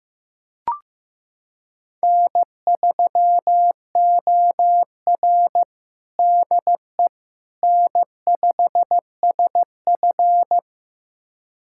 Morse code, heard as a keyed tone, about 15 wpm, 700 Hz. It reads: N3OR DE N5SF
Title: USA Callsign DE Callsign - 4 Characters - 15wpm